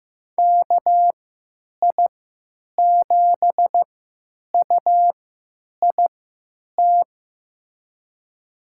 Morse code, heard as a keyed tone, about 15 wpm, spaced as for 9 wpm, 700 Hz. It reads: KI7UIT
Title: USA Call Signs - 6 Characters - 3x Character Spacing - 15wpm